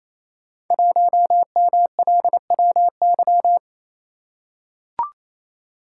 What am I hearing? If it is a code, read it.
Morse code, heard as a keyed tone, about 29 wpm, 700 Hz. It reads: 1MLWY
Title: Code Groups - 5 Letter:Numbers - 28wpm